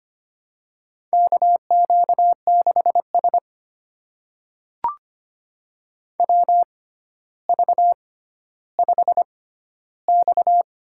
Morse code, heard as a keyed tone, about 25 wpm, spaced as for 13 wpm, 700 Hz. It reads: KQ6S WV5X
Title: USA Call Signs - 4 Characters - 6x Character Spacing - 25wpm